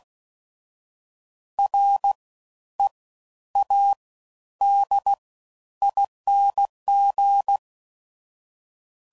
Morse code, heard as a keyed tone, about 16 wpm, spaced as for 10 wpm, 800 Hz. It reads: READING
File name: Suffixes / Special INGs / Fast Ings